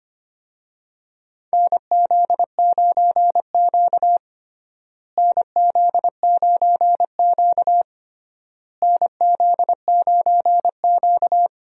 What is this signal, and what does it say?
Morse code, heard as a keyed tone, about 25 wpm, 700 Hz. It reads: NZ9Q NZ9Q NZ9Q
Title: USA Call Signs - 4 Characters - Repeated 3x - 25wpm